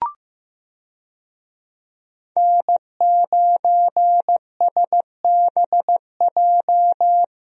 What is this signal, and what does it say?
Morse code, heard as a keyed tone, about 15 wpm, 700 Hz. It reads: N9SBJ
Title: USA Call Signs - 5 Characters - Repeated 3x - 15wpm